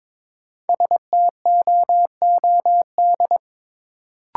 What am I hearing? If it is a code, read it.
Morse code, heard as a keyed tone, about 22 wpm, 700 Hz. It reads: STOOD
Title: Sentences from Top 500 Words - New Word Drill - 22wpm